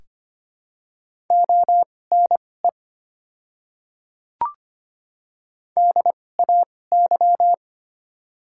Morse code, heard as a keyed tone, about 25 wpm, spaced as for 18 wpm, 700 Hz. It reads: ONE DAY
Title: 2x Character Spacing - Top 100 Words - 25wpm